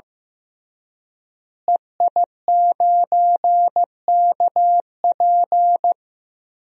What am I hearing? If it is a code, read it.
Morse code, heard as a keyed tone, about 16 wpm, 700 Hz. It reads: EI9KP
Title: DXCC-100 Call Signs - 5 Characters - Repeated 2x - 15wpm